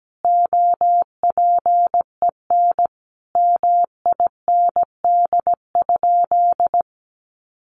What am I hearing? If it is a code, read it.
Morse code, heard as a keyed tone, about 17 wpm, 700 Hz. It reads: OPEN MIND?
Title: Sentences from Top 1300 Words 17wpm